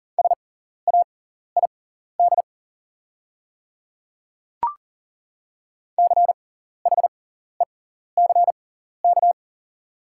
Morse code, heard as a keyed tone, about 37 wpm, spaced as for 14 wpm, 700 Hz. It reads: SAID CHECK
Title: 6x Character Spacing - Top 500 Words - 40wpm